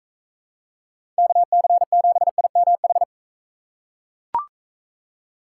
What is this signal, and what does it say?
Morse code, heard as a keyed tone, about 38 wpm, 700 Hz. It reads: KC7IMH